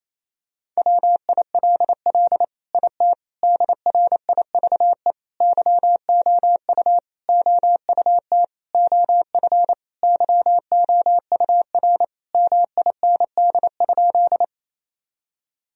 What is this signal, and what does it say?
Morse code, heard as a keyed tone, about 28 wpm, 700 Hz. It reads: WILL IT DRIVE YOU OUT OF YOUR MIND?
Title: Sentences from Top 1000 Words - Review Part 2 - 28wpm